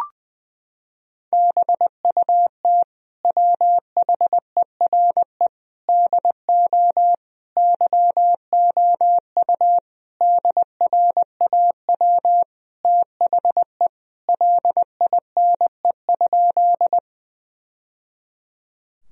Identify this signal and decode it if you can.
Morse code, heard as a keyed tone, about 20 wpm, 700 Hz. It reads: BUT WHERE DO YOU DRAW THE LINE?
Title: Sentences from Top 1000 Words - Review Part 1 - 20wpm